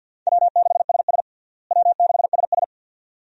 Morse code, heard as a keyed tone, about 48 wpm, 700 Hz. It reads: W6SS W6SS
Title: USA Call Signs - 4 Characters - Repeated 2x - 50wpm